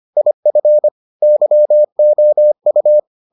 Morse code, heard as a keyed tone, about 25 wpm, 600 Hz. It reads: IF YOU